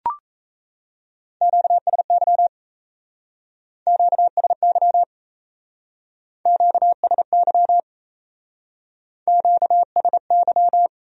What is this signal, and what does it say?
Morse code, heard as a keyed tone, about 34 wpm, 700 Hz. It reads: QSY QSY QSY QSY
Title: Speed-Racing Q-Codes - 28wpm